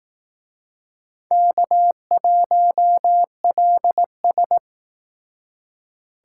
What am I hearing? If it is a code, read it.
Morse code, heard as a keyed tone, about 18 wpm, 700 Hz. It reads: K1LS
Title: Speed Racing - USA 1x2 Call Signs 15wpm